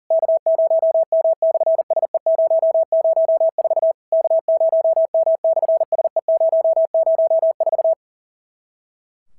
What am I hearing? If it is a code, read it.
Morse code, heard as a keyed tone, about 42 wpm, 650 Hz. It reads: K0M/SE004 K0M/SE004
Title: SOTA QSO Practice 40wpm